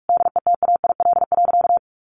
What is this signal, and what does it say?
Morse code, heard as a keyed tone, about 44 wpm, 700 Hz. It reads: DETAIL.
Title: Sentences from Top 1500 Words 45wpm